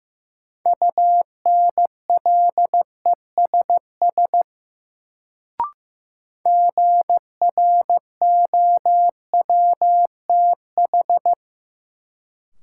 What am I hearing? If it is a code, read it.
Morse code, heard as a keyed tone, about 15 wpm, 700 Hz. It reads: UNLESS GROWTH
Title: Sentences from Top 1300 Words - New Word Drill Alt - 15wpm